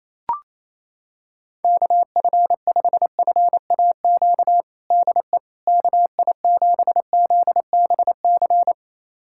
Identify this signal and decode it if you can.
Morse code, heard as a keyed tone, about 28 wpm, 700 Hz. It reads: KF5FAQ DE KI7ZBC
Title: USA Callsign DE Callsign 28wpm